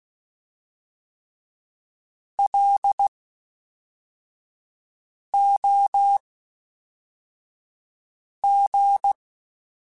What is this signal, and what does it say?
Morse code, heard as a keyed tone, about 16 wpm, spaced as for 4 wpm, 800 Hz. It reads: LOG
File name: (spaced) / FromSMs spaced mp3s / Antennas sp@5